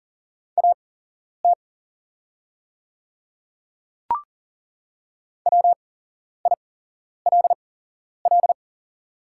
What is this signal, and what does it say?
Morse code, heard as a keyed tone, about 40 wpm, spaced as for 11 wpm, 690 Hz. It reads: AT WILL
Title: 8x Character Spacing - Top 100 Words - 40wpm